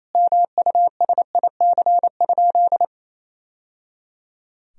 Morse code, heard as a keyed tone, about 28 wpm, 700 Hz. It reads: MUSIC?
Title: Sentences from Top 1000 Words - Review Part 1 - 28wpm